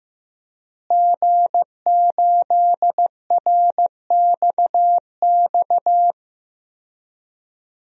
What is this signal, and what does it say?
Morse code, heard as a keyed tone, about 15 wpm, 700 Hz. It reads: G8RXX
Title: DXCC-100 Call Signs - 5 Characters - Repeated 2x - 15wpm